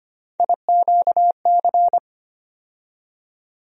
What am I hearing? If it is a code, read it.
Morse code, heard as a keyed tone, about 25 wpm, 700 Hz. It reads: IQC
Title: USA Call Sign Postfixes - 3 Letters 25wpm